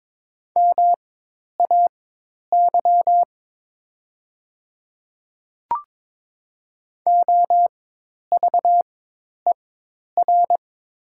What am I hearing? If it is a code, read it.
Morse code, heard as a keyed tone, about 22 wpm, spaced as for 10 wpm, 700 Hz. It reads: MAY OVER
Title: 4x Character Spacing - Top 100 Words - 22wpm